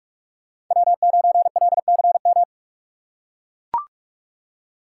Morse code, heard as a keyed tone, about 44 wpm, 700 Hz. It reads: W9LCM
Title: USA Call Signs - 5 Characters - 6x Character Spacing - 45wpm